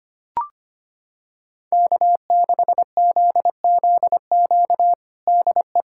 Morse code, heard as a keyed tone, about 25 wpm, 700 Hz. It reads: K6ZZQ DE
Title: USA Callsign DE Callsign 25wpm